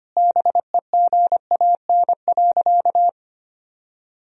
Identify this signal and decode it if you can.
Morse code, heard as a keyed tone, about 25 wpm, 700 Hz. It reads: BEGAN.
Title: Sentences from Top 1000 Words - Review Part 1 - 25wpm